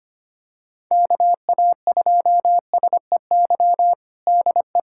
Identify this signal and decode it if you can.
Morse code, heard as a keyed tone, about 25 wpm, 700 Hz. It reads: KA2SEY DE
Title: USA Callsign DE Callsign 25wpm